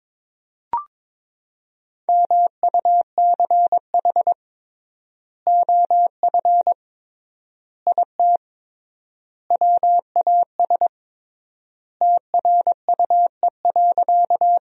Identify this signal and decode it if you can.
Morse code, heard as a keyed tone, about 22 wpm, 700 Hz. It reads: MUCH OF IT WAS TRUE.
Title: 3x Word-Spacing - Sentences from Top 400 Words - 22wpm